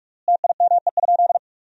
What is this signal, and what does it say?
Morse code, heard as a keyed tone, about 46 wpm, 700 Hz. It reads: TIME?
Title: 3x Word-Spacing - Sentences from Top 200 Words - 45wpm